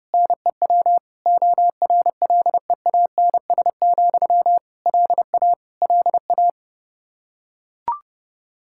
Morse code, heard as a keyed tone, about 32 wpm, 700 Hz. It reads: NEW ORLEANS, LA LA
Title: Practice QSO - Exchange 1 - Beginning - 30wpm